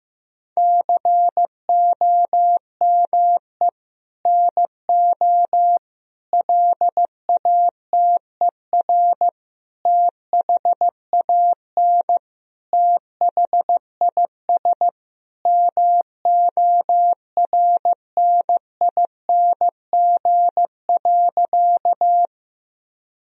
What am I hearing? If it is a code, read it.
Morse code, heard as a keyed tone, about 15 wpm, 700 Hz. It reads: COME NO LATER THAN THIS MORNING.
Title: Sentences from Top 1000 Words - Review Part 4 - 15wpm